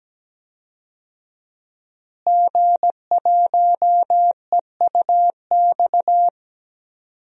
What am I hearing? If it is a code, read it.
Morse code, heard as a keyed tone, about 17 wpm, 700 Hz. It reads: G1EUX
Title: DXCC-100 Call Signs - 5 Characters - Repeated 2x - 17wpm